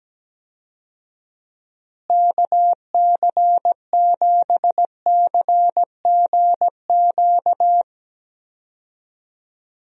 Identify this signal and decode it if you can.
Morse code, heard as a keyed tone, about 17 wpm, 700 Hz. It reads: KC7CGQ
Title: USA Call Signs - 6 Characters - Repeated 2x - 17wpm